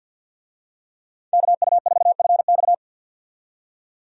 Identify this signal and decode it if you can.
Morse code, heard as a keyed tone, about 47 wpm, 700 Hz. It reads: KU4FX
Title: USA Call Signs - 5 Characters - 50wpm